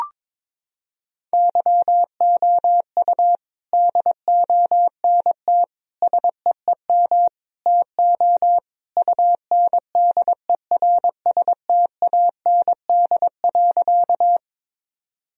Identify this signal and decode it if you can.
Morse code, heard as a keyed tone, about 21 wpm, 700 Hz. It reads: YOU DONT SEEM TO UNDERSTAND.
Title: Sentences from Top 1000 Words - Review Part 1 - 22wpm